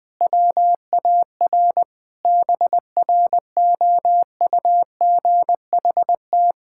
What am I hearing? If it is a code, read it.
Morse code, heard as a keyed tone, about 20 wpm, 700 Hz. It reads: WAR BROUGHT
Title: Sentences with 5-Word Phrases - 20wpm